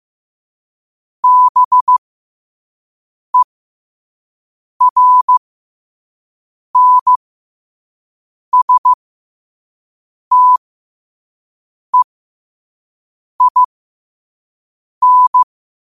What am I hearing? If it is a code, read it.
Morse code, heard as a keyed tone, about 15 wpm, spaced as for 5 wpm, 1000 Hz. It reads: BERNSTEIN
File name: qotdfiles / 5wpm